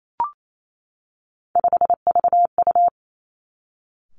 Morse code, heard as a keyed tone, about 28 wpm, 700 Hz. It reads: 5VU
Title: Three Challenging Letter-Number Combinations 28wpm